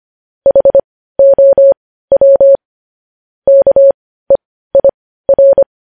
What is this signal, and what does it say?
Morse code, heard as a keyed tone, about 25 wpm, spaced as for 15 wpm, 550 Hz. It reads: HOW KEIR
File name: podcast / News headlines-1x-25x15